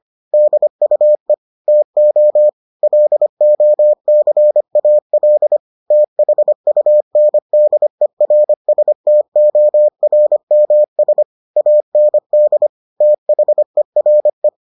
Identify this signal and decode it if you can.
Morse code, heard as a keyed tone, about 25 wpm, 600 Hz. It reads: DUE TO LOCAL THUNDERSTORMS AND THERE